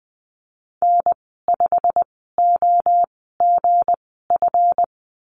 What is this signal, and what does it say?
Morse code, heard as a keyed tone, about 20 wpm, spaced as for 15 wpm, 700 Hz. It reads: N5OGF
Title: 193 Lesson 31g - Letter v - Call Signs 20wpm